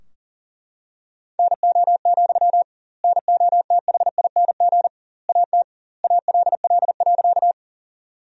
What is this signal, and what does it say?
Morse code, heard as a keyed tone, about 37 wpm, 700 Hz. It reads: NO, NOTHING AT ALL.
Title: 2x Word-Spacing - Sentences from Top 500 Words - 40wpm